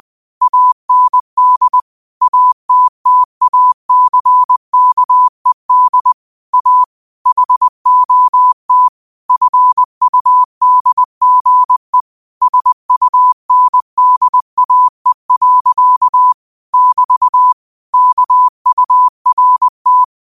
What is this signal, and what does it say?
Morse code, heard as a keyed tone, about 20 wpm, 1000 Hz. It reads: AND ATTACKED A HOT FUDGE SUNDAE. <BT> KURT